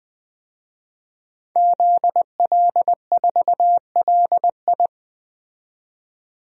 Morse code, heard as a keyed tone, about 20 wpm, 700 Hz. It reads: ZL4LI
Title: DXCC-100 Call Signs - 5 Characters - 20wpm